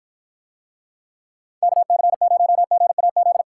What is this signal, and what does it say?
Morse code, heard as a keyed tone, about 52 wpm, 700 Hz. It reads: KC0GAZ